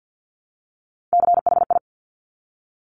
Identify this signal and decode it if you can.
Morse code, heard as a keyed tone, about 50 wpm, 700 Hz. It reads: CHI
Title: Top Three Letter Combinations in Words 50wpm